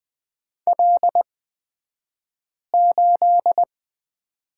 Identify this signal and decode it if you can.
Morse code, heard as a keyed tone, about 20 wpm, spaced as for 5 wpm, 700 Hz. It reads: L8
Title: 254 Lesson 41d - Period - Two Characters 20wpm